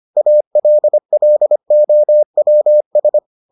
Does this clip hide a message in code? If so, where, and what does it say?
Yes: Morse code from 0.2 to 3.2 s, heard as a keyed tone, about 25 wpm, 600 Hz. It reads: ALLOWS